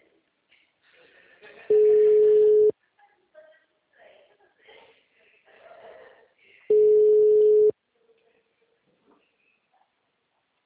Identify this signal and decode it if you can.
Ringback tone (Europe, 425 Hz).